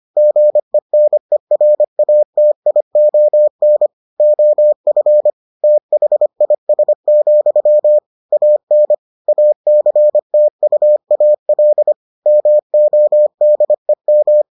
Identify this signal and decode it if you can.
Morse code, heard as a keyed tone, about 25 wpm, 600 Hz. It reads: GENERATION OF THIS, AN ACTUAL MODEM